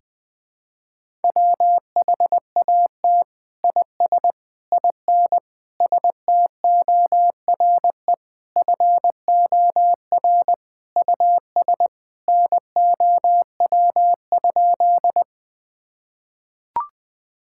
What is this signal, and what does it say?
Morse code, heard as a keyed tone, about 21 wpm, 700 Hz. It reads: WHAT IS IN STORE FOR US NOW?
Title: Sentences from Top 1000 Words - Review Part 5 - 20wpm